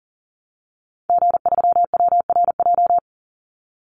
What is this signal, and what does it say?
Morse code, heard as a keyed tone, about 40 wpm, 700 Hz. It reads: G3WRJ
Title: Call Signs - 40wpm